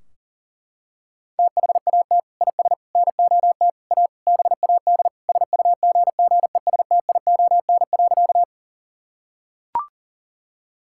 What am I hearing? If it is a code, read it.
Morse code, heard as a keyed tone, about 40 wpm, 700 Hz. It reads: THAT IS NOT A BAD SUGGESTION.
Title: Sentences from Top 1600 Words 40wpm